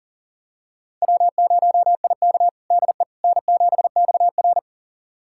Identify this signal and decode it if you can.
Morse code, heard as a keyed tone, about 40 wpm, 700 Hz. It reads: W0IK DE N7XR
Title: USA Callsign DE Callsign - 4 Characters - 40wpm